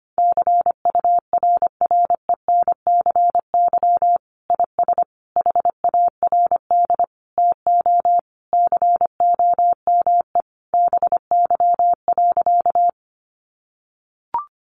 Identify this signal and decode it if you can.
Morse code, heard as a keyed tone, about 25 wpm, 700 Hz. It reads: CURRENCY IS HARD TO COME BY.
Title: Sentences from Top 1500 Words 25wpm